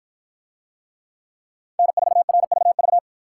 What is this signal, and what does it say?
Morse code, heard as a keyed tone, about 53 wpm, 700 Hz. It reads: N4RVV